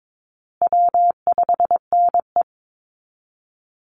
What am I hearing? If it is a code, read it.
Morse code, heard as a keyed tone, about 22 wpm, 700 Hz. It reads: W5NE
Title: Easy Call Signs Repeated 2x - 22wpm